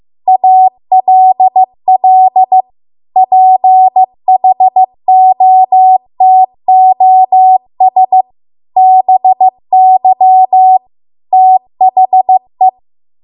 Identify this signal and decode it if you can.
Morse code, heard as a keyed tone, about 15 wpm, 750 Hz. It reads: ALL PHOTOS BY THE
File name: Archive / 15 WPM / 15WPM